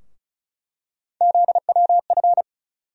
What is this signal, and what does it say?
Morse code, heard as a keyed tone, about 36 wpm, 700 Hz. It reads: ZWF
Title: USA Call Sign Postfixes - 3 Letters 35wpm